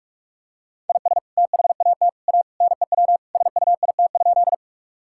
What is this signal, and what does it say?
Morse code, heard as a keyed tone, about 42 wpm, 700 Hz. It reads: IS THAT A NEW SUIT?
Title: Sentences from Top 1600 Words 45wpm